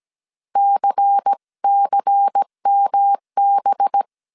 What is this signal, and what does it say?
Morse code, heard as a keyed tone, about 17 wpm, 800 Hz. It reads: CCMB
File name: CW / DET / 1 9 1500C S C 16wpm 2909